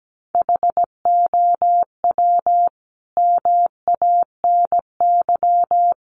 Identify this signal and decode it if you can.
Morse code, heard as a keyed tone, about 17 wpm, 700 Hz. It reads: HOW MANY
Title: Sentences from Top 300 Words 17wpm